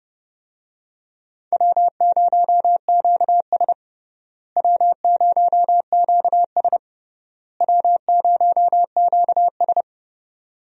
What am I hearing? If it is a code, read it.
Morse code, heard as a keyed tone, about 30 wpm, 700 Hz. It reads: W0QS W0QS W0QS
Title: USA Call Signs - 4 Characters - Repeated 3x - 30wpm